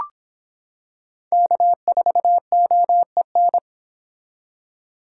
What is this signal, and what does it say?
Morse code, heard as a keyed tone, about 26 wpm, 700 Hz. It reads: K4OEN